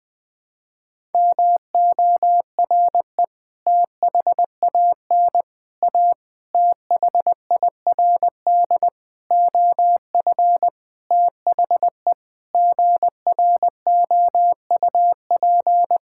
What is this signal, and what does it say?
Morse code, heard as a keyed tone, about 20 wpm, 700 Hz. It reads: MORE THAN A THIRD OF THE GROUP
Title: Sentences with 6-Word Phrases - 20wpm